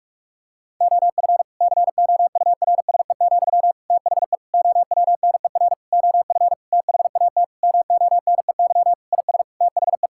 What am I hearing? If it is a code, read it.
Morse code, heard as a keyed tone, about 46 wpm, 700 Hz. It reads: OF COURSE, THE OWNER OF THAT MONEY IS THE
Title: Sentences from Top 1600 Words 45wpm